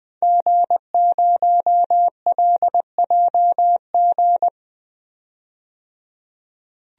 Morse code, heard as a keyed tone, about 20 wpm, 700 Hz. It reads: G0LJG
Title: DXCC-100 Call Signs - 5 Characters - 20wpm